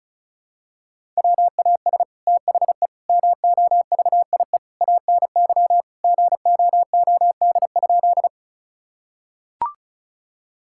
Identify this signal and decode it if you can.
Morse code, heard as a keyed tone, about 34 wpm, 700 Hz. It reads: WAS THE MOVIE ANY GOOD?
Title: Sentences from Top 1000 Words - Review Part 5 - 35wpm